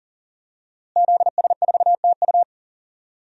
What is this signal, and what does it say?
Morse code, heard as a keyed tone, about 40 wpm, 700 Hz. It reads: ZS4TU
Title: DXCC-100 Call Signs - 5 Characters - Repeated 3x - 40wpm